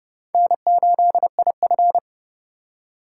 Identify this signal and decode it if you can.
Morse code, heard as a keyed tone, about 29 wpm, 700 Hz. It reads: N8IF